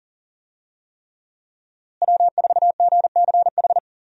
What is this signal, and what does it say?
Morse code, heard as a keyed tone, about 40 wpm, 700 Hz. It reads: W4GCH
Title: USA Call Signs - 5 Characters - Repeated 3x - 40wpm